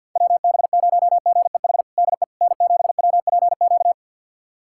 Morse code, heard as a keyed tone, about 51 wpm, 700 Hz. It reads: WB0GEH DE N7WPQ
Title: USA Callsign DE Callsign 50wpm